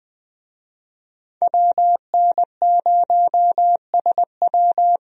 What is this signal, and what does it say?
Morse code, heard as a keyed tone, about 20 wpm, 700 Hz. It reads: WN0SW
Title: USA Call Signs - 5 Characters - 20wpm